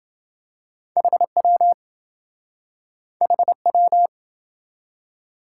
Morse code, heard as a keyed tone, about 28 wpm, 700 Hz. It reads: HW HW